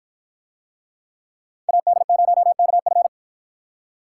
Morse code, heard as a keyed tone, about 48 wpm, 700 Hz. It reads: AD0KF